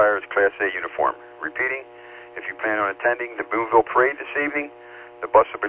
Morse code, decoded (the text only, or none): none